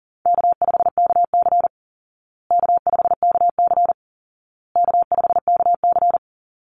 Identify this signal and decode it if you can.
Morse code, heard as a keyed tone, about 40 wpm, 700 Hz. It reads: K5KC K5KC K5KC
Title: Easy Call Signs Repeated 3x - 40wpm